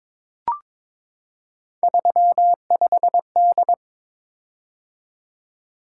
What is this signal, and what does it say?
Morse code, heard as a keyed tone, about 22 wpm, 700 Hz. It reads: 35D
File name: practice / 22wpm / Three Letter-Number Combinations - 22wpm